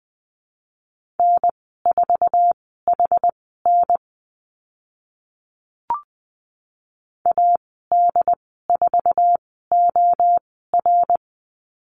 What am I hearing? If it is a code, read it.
Morse code, heard as a keyed tone, about 20 wpm, spaced as for 14 wpm, 700 Hz. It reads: N4HN AD4OR
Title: 076 Lesson 13g - Letter L - Call Signs 20wpm